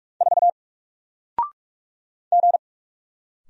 Morse code, heard as a keyed tone, about 44 wpm, 700 Hz. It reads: 4 G
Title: Single Character (Letter:Number:Punctuation:Prosign) 45wpm